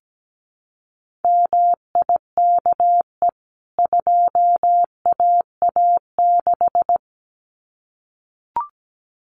Morse code, heard as a keyed tone, about 17 wpm, 700 Hz. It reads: MIKE 2AA6
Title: CWT Exchange 17wpm